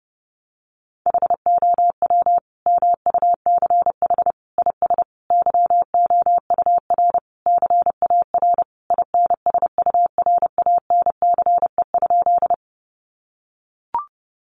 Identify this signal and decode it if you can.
Morse code, heard as a keyed tone, about 30 wpm, 700 Hz. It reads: HOW MUCH IS YOUR CAR INSURANCE?
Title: Sentences from Top 1100 Words 30wpm